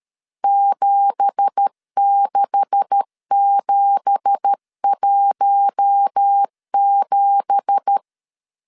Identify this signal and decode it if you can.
Morse code, heard as a keyed tone, about 13 wpm, 800 Hz. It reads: 76717
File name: LEZIONI / 9 2867 12wpm 2322